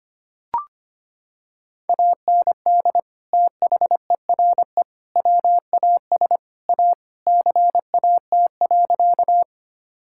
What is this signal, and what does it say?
Morse code, heard as a keyed tone, about 25 wpm, 700 Hz. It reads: AND THERE WAS A CAT.
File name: practice / 25wpm / Sentences with 4-Word Phrases - 25wpm